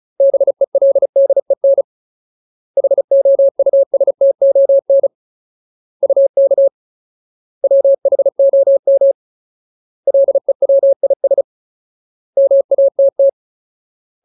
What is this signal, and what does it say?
Morse code, heard as a keyed tone, about 35 wpm, 550 Hz. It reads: BELDEN HOUSTON UK WHOM LEWIS MATT
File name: Ultimate Word List / 35WPM Ultimate